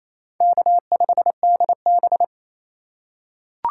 Morse code, heard as a keyed tone, about 28 wpm, 700 Hz. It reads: K5DB
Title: USA Call Signs - 4 Characters - 3x Character Spacing - 28wpm